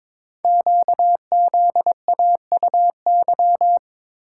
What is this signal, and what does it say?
Morse code, heard as a keyed tone, about 22 wpm, 700 Hz. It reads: QZAUY
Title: Code Groups - 5 Letters - 22wpm